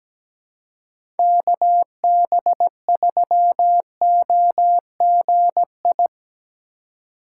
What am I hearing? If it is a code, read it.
Morse code, heard as a keyed tone, about 17 wpm, 700 Hz. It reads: KB3OGI
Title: USA Call Signs - 6 Characters - 4x Character Spacing - 17wpm